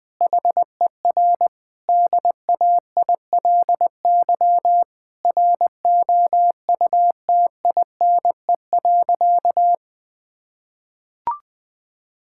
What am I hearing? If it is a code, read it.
Morse code, heard as a keyed tone, about 20 wpm, 700 Hz. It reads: HER DAILY ROUTINE.
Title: Difficult Sentences with Hints - 20wpm